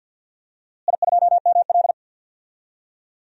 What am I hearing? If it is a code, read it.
Morse code, heard as a keyed tone, about 52 wpm, 700 Hz. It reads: I2ML
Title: DXCC-100 Call Signs - 4 Characters - Repeated 2x - 50wpm